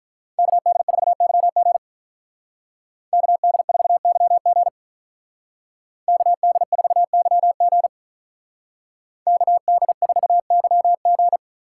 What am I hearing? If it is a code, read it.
Morse code, heard as a keyed tone, about 43 wpm, 700 Hz. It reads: KD4YG KD4YG KD4YG KD4YG
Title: Speed Racing - USA 2x2 Call Signs 35wpm